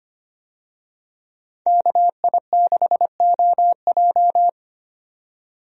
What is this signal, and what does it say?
Morse code, heard as a keyed tone, about 25 wpm, 700 Hz. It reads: KI6OJ
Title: USA Call Signs - 5 Characters - 25wpm